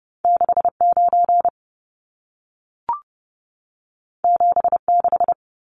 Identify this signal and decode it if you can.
Morse code, heard as a keyed tone, about 30 wpm, 700 Hz. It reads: 69 76
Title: Numbers - 2-Digit 30wpm